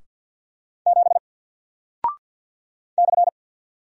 Morse code, heard as a keyed tone, about 48 wpm, 700 Hz. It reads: 7 /
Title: Single Character (Letter:Number:Punctuation:Prosign) Alt 50wpm